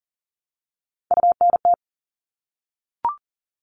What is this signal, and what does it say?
Morse code, heard as a keyed tone, about 40 wpm, 700 Hz. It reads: UNT
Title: All Cut Numbers - 3-Digit 40wpm